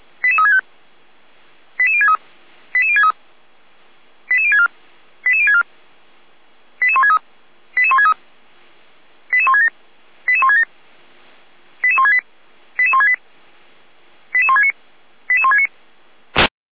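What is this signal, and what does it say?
Five-tone selective call (ZVEI), heard as quick runs of five tones, tones of about 70 ms. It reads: 88365 88063 88063 88064 88064 88163 88163 88167 88167 88168 88168 88169 88169
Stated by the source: Five-tone ZVEI selective call, code 88352, received by SDR.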